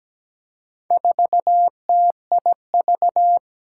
Morse code, heard as a keyed tone, about 17 wpm, 700 Hz. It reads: 4TIV
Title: Four Challenging Letter-Number Combinations 17wpm